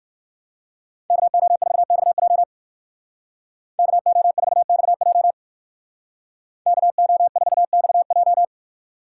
Morse code, heard as a keyed tone, about 55 wpm, 700 Hz. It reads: KO4XJ KO4XJ KO4XJ
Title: Speed Racing - USA 2x2 Call Signs 45wpm